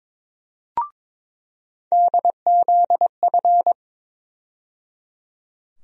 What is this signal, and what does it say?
Morse code, heard as a keyed tone, about 22 wpm, 700 Hz. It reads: DZF
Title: USA Call Sign Postfixes - 3 Letters 22wpm